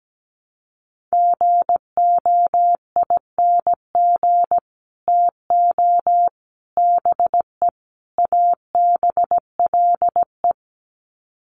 Morse code, heard as a keyed tone, about 17 wpm, 700 Hz. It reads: GOING TO BE ABLE
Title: Sets of 4 Words 17wpm